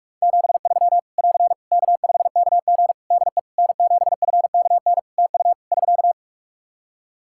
Morse code, heard as a keyed tone, about 46 wpm, 700 Hz. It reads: 73 <AR> K5KG DE N7FKN TU <SK>
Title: K1USN SST QSO Practice 45wpm